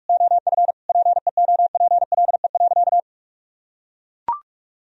Morse code, heard as a keyed tone, about 41 wpm, 700 Hz. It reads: OF PEOPLE.
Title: Sentences from Top 1000 Words - Review Part 3 - 45wpm